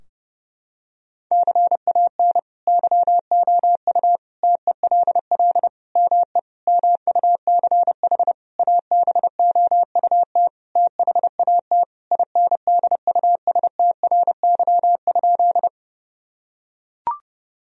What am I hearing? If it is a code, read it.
Morse code, heard as a keyed tone, about 30 wpm, 700 Hz. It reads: CAN YOU TELL ME MUCH ABOUT THAT INDUSTRY?
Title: Sentences from Top 1000 Words - Review Part 4 - 30wpm